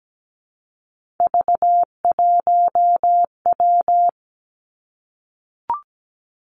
Morse code, heard as a keyed tone, about 17 wpm, 700 Hz. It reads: V1W
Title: Three Challenging Letter-Number Combinations 17wpm